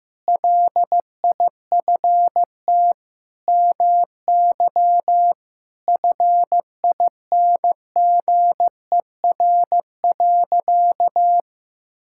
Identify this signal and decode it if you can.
Morse code, heard as a keyed tone, about 15 wpm, 700 Hz. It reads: LIFT MY FINGER.
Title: Sentences from Top 1600 Words 15wpm